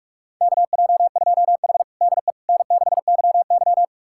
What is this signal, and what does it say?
Morse code, heard as a keyed tone, about 46 wpm, 700 Hz. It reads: KJ2H DE N6YY
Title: USA Callsign DE Callsign - 4 Characters - 45wpm